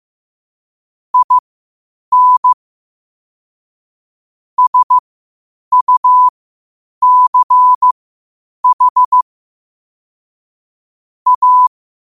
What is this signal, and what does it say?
Morse code, heard as a keyed tone, about 15 wpm, spaced as for 8 wpm, 1000 Hz. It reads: IN SUCH A
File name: qotdfiles / 8wpm